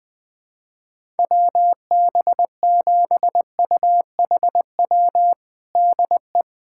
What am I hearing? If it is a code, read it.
Morse code, heard as a keyed tone, about 21 wpm, 700 Hz. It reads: WB7UHW DE
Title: USA Callsign DE Callsign 20wpm